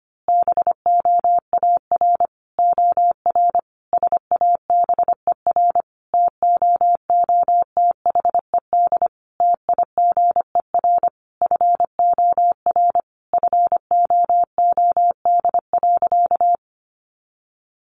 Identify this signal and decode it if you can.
Morse code, heard as a keyed tone, about 25 wpm, 700 Hz. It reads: BOAR OR SABER TOOTHED TIGER FOR FOOD.
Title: A Sound Of Thunder 25wpm